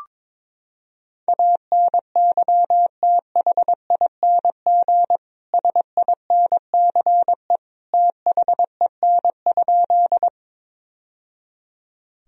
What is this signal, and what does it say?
Morse code, heard as a keyed tone, about 21 wpm, 700 Hz. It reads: ANYTHING SINCE THEN?
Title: Sentences from Top 1000 Words - Review Part 1 - 22wpm